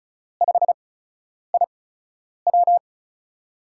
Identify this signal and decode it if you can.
Morse code, heard as a keyed tone, about 36 wpm, 700 Hz. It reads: 5 I W
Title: USA Call Signs - 4 Characters - 8x Character Spacing - 35wpm